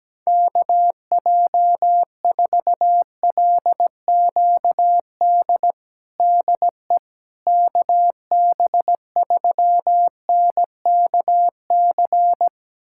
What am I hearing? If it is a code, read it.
Morse code, heard as a keyed tone, about 17 wpm, 700 Hz. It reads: KJ4LQD DE KB3NKC
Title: USA Callsign DE Callsign - 6 Characters - 17wpm